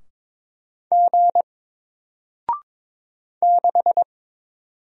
Morse code, heard as a keyed tone, about 22 wpm, 700 Hz. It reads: G 6
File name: Single-Letter-Number-22wpm